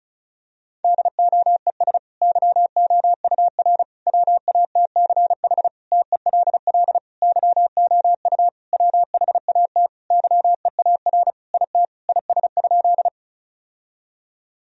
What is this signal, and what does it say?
Morse code, heard as a keyed tone, about 34 wpm, 700 Hz. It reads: DOES YOUR WATCH TELL YOU WHAT YEAR IT IS?
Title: Sentences from Top 1000 Words - Review Part 1 - 35wpm